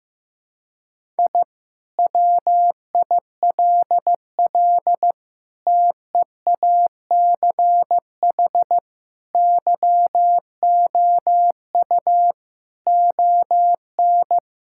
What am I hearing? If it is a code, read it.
Morse code, heard as a keyed tone, about 15 wpm, 700 Hz. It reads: I WILL TEACH YOU ON
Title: Sentences from Top 1000 Words - Review Part 6 - 15wpm